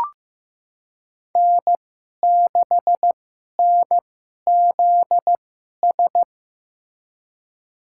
Morse code, heard as a keyed tone, about 15 wpm, spaced as for 11 wpm, 700 Hz. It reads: N6NZS
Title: USA Call Signs - 5 Characters - 2x Character Spacing - 15wpm